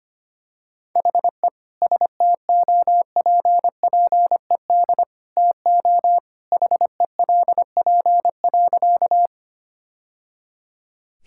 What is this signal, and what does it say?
Morse code, heard as a keyed tone, about 25 wpm, 700 Hz. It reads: HE STOPPED TO HELP.
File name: practice / 25wpm / Sentences with 2-Word Phrases - 25wpm